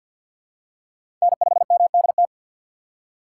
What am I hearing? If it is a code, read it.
Morse code, heard as a keyed tone, about 52 wpm, 700 Hz. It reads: N5MDT
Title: USA Call Signs - 5 Characters - 2x Character Spacing - 50wpm